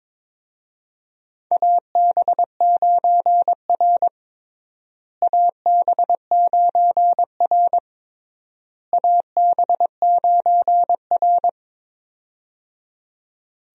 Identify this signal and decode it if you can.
Morse code, heard as a keyed tone, about 22 wpm, 700 Hz. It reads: AB9R AB9R AB9R
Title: USA Call Signs - 4 Characters - Repeated 3x - 22wpm